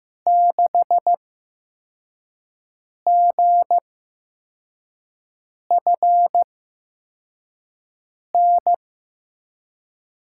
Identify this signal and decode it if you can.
Morse code, heard as a keyed tone, about 15 wpm, spaced as for 4 wpm, 700 Hz. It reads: 6GFN
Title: USA Call Signs - 6 Characters - 8x Character Spacing - 15wpm